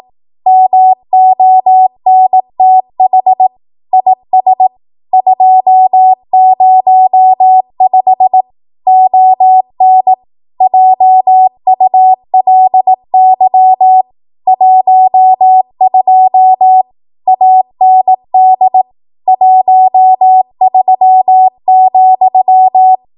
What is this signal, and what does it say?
Morse code, heard as a keyed tone, about 18 wpm, 750 Hz. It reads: MONTH IS 205 ON JULY 12 AND 13,
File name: Archive / 18 WPM / 18WPM